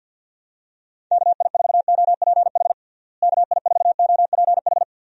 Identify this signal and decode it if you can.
Morse code, heard as a keyed tone, about 52 wpm, 700 Hz. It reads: KI4OPH KI4OPH
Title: USA Call Signs - 6 Characters - Repeated 2x - 50wpm